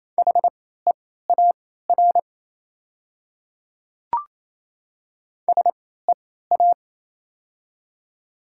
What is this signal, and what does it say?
Morse code, heard as a keyed tone, about 28 wpm, spaced as for 16 wpm, 700 Hz. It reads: HEAR SEA
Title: 3x Character Spacing - Top 300 Words - 28wpm